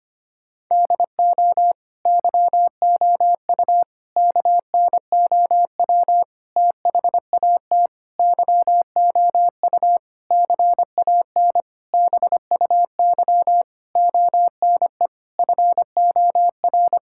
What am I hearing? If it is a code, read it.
Morse code, heard as a keyed tone, about 25 wpm, 700 Hz. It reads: DO YOU KNOW THAT YOU CAN BUY ONE FOR
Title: Sentences from Top 1000 Words - Review Part 5 - 25wpm